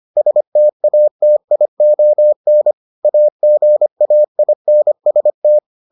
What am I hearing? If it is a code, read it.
Morse code, heard as a keyed tone, about 25 wpm, 600 Hz. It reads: STATION AGAINST